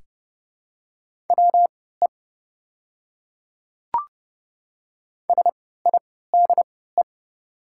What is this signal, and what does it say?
Morse code, heard as a keyed tone, about 30 wpm, spaced as for 17 wpm, 700 Hz. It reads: WE SIDE
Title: 3x Character Spacing - Top 100 Words - 30wpm